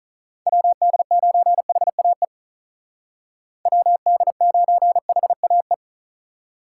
Morse code, heard as a keyed tone, about 46 wpm, 700 Hz. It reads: WD9HAE WD9HAE